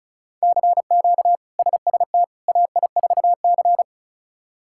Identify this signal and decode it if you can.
Morse code, heard as a keyed tone, about 34 wpm, 700 Hz. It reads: CQ SST AI4C
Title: K1USN SST QSO Practice - 35wpm